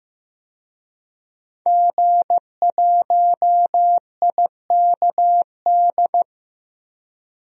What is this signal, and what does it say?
Morse code, heard as a keyed tone, about 15 wpm, 700 Hz. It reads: G1IKD
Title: DXCC-100 Call Signs - 5 Characters - 15wpm